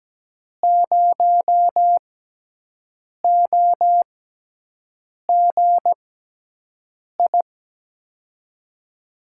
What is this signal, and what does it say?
Morse code, heard as a keyed tone, about 17 wpm, spaced as for 6 wpm, 700 Hz. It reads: 0OGI
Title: USA Call Signs - 5 Characters - 6x Character Spacing - 17wpm